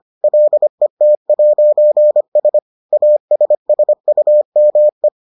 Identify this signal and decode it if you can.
Morse code, heard as a keyed tone, about 25 wpm, 600 Hz. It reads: LET'S ASSUME